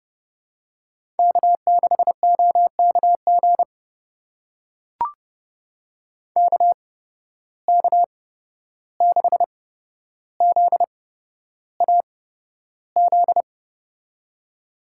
Morse code, heard as a keyed tone, about 32 wpm, spaced as for 12 wpm, 700 Hz. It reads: K6OKG KK6ZAZ
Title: USA Call Signs - 8x Character Spacing - 30wpm